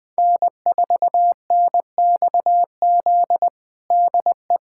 Morse code, heard as a keyed tone, about 19 wpm, 700 Hz. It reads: N4NXZ DE
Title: USA Callsign DE Callsign 20wpm